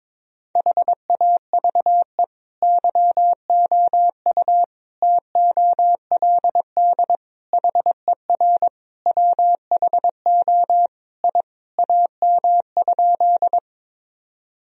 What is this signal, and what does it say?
Morse code, heard as a keyed tone, about 22 wpm, 700 Hz. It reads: HAVE YOU TOLD HER WHO I AM?
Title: Sentences from Top 1000 Words - Review Part 2 - 22wpm